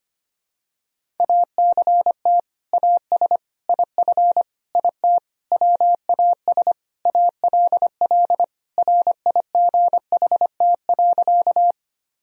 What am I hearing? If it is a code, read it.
Morse code, heard as a keyed tone, about 25 wpm, 700 Hz. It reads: ACT AS IF IT WAS ALL RIGHT.
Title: Sentences from Top 1000 Words - Review Part 1 - 25wpm